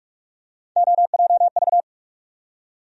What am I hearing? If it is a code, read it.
Morse code, heard as a keyed tone, about 46 wpm, 700 Hz. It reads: OJV
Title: USA Call Sign Postfixes - 3 Letters 45wpm